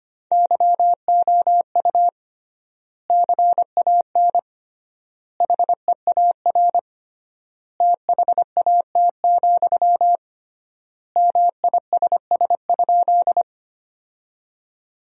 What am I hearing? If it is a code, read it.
Morse code, heard as a keyed tone, about 25 wpm, 700 Hz. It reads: YOU CAN HEAR THAT, MISS?
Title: 3x Word-Spacing - Sentences from Top 500 Words - 25wpm